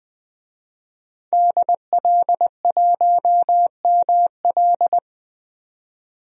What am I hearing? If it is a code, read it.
Morse code, heard as a keyed tone, about 20 wpm, 700 Hz. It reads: DL1ML
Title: DXCC-100 Call Signs - 5 Characters - 20wpm